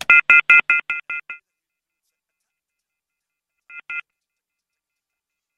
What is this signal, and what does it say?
Off-hook warning tone, 1400 + 2060 + 2450 + 2600 Hz.